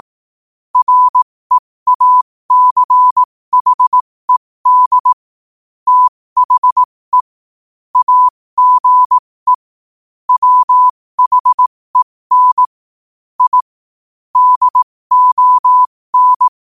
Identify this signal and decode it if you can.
Morse code, heard as a keyed tone, about 18 wpm, spaced as for 15 wpm, 1000 Hz. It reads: REACHED THE AGE WHEN I DON
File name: qotdfiles / 15wpm